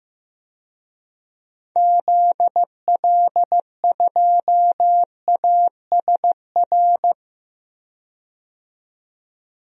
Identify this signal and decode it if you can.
Morse code, heard as a keyed tone, about 15 wpm, 700 Hz. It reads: ZL2ASR
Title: DXCC-100 Call Signs - 6 Characters - 15wpm